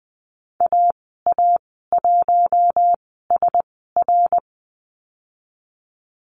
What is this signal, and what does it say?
Morse code, heard as a keyed tone, about 20 wpm, spaced as for 14 wpm, 700 Hz. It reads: AA1SR
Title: 122 Lesson 20g - Number 3 - Call Signs 20wpm